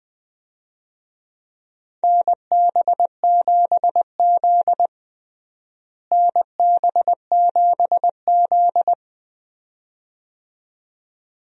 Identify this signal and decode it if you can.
Morse code, heard as a keyed tone, about 21 wpm, 700 Hz. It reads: NB7Z NB7Z
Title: USA Call Signs - 4 Characters - Repeated 2x - 20wpm